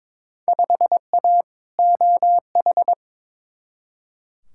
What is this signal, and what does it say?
Morse code, heard as a keyed tone, about 22 wpm, 700 Hz. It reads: 5A OH
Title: Field Day Exchanges Encore - 22wpm